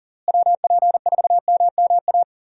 Morse code, heard as a keyed tone, about 40 wpm, 700 Hz. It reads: WP4MMA
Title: USA Call Signs - 6 Characters - Repeated 2x - 40wpm